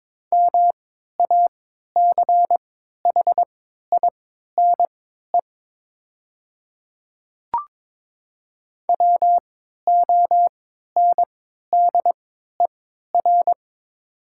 Morse code, heard as a keyed tone, about 22 wpm, spaced as for 12 wpm, 700 Hz. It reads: MACHINE WONDER
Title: 3x Character Spacing - Top 500 Words - 22wpm